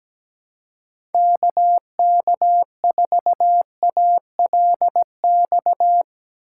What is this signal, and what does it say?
Morse code, heard as a keyed tone, about 17 wpm, 700 Hz. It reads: KK4ALX
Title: USA Call Signs - 6 Characters - 2x Character Spacing - 17wpm